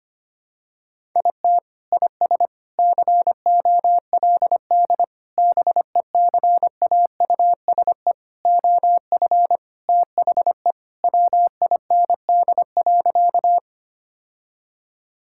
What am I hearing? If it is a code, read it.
Morse code, heard as a keyed tone, about 25 wpm, 700 Hz. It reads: IT IS COLD BECAUSE OF THE WIND.
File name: practice / 25wpm / Sentences with 3-Word Phrases - 25wpm